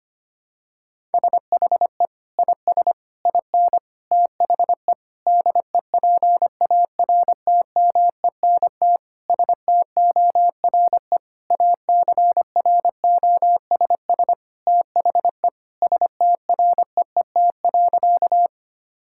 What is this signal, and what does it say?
Morse code, heard as a keyed tone, about 25 wpm, 700 Hz. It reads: SHE IS IN THE DEPARTMENT STORE ACROSS THE STREET.
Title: Sentences from Top 1000 Words - Review Part 5 - 25wpm